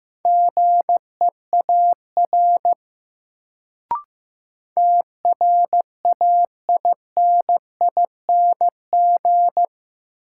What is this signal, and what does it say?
Morse code, heard as a keyed tone, about 15 wpm, 700 Hz. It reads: GEAR TRAINING
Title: Sentences from Top 1400 Words - New Word Drill Alt - 15wpm